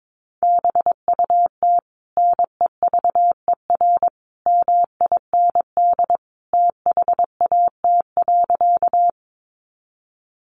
Morse code, heard as a keyed tone, about 22 wpm, 700 Hz. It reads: BUT NEVER MIND THAT.
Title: Sentences from Top 500 Words 22wpm